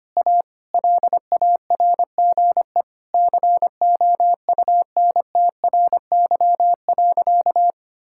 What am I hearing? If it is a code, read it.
Morse code, heard as a keyed tone, about 25 wpm, 700 Hz. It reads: A LARGE COUNTRY.
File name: practice / 25wpm / Sentences with 4-Word Phrases - 25wpm